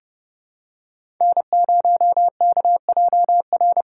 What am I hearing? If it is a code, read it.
Morse code, heard as a keyed tone, about 28 wpm, 700 Hz. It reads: N0KJR